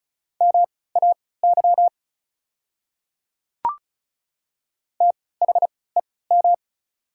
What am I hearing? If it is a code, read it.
Morse code, heard as a keyed tone, about 36 wpm, spaced as for 20 wpm, 690 Hz. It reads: MAY THEM
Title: 3x Character Spacing - Top 100 Words - 35wpm